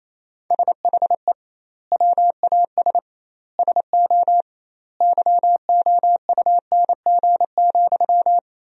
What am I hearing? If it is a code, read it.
Morse code, heard as a keyed tone, about 28 wpm, 700 Hz. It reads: SHE WAS SO YOUNG,
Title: 2x Word-Spacing - Sentences from Top 400 Words - 28wpm